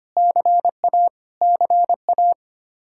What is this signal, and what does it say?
Morse code, heard as a keyed tone, about 25 wpm, 700 Hz. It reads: CA CA
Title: K1USN SST QSO Practice 25wpm